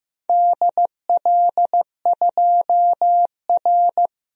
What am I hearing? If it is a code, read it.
Morse code, heard as a keyed tone, about 15 wpm, 700 Hz. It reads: DL2R
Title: DXCC-100 Call Signs - 4 Characters - 15wpm